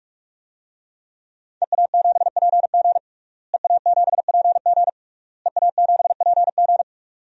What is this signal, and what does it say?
Morse code, heard as a keyed tone, about 46 wpm, 700 Hz. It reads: EA7PG EA7PG EA7PG
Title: DXCC-100 Call Signs - 5 Characters - Repeated 3x - 45wpm